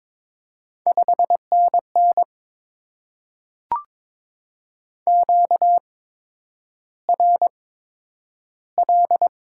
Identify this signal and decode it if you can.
Morse code, heard as a keyed tone, about 22 wpm, spaced as for 9 wpm, 700 Hz. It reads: 5NN QRL
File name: Top 20 QSO Elements - 8x Character Spacing - 22wpm